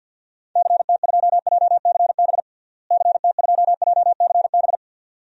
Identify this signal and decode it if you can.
Morse code, heard as a keyed tone, about 48 wpm, 700 Hz. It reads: CT2JCB CT2JCB
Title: DXCC-100 Call Signs - 6 Characters - Repeated 2x - 50wpm